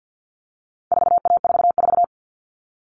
Morse code, heard as a keyed tone, about 50 wpm, 700 Hz. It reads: 4A44
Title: All Cut Numbers - 4-Digit 50wpm